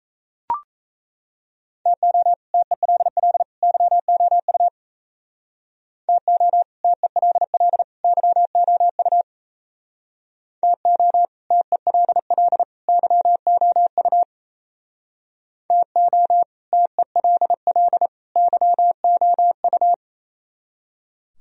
Morse code, heard as a keyed tone, about 36 wpm, 700 Hz. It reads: TO TELL YOU TO TELL YOU TO TELL YOU TO TELL YOU